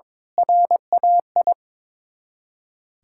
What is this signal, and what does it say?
Morse code, heard as a keyed tone, about 22 wpm, 700 Hz. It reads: RAI
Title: Ontario QSO Party Exchange 22wpm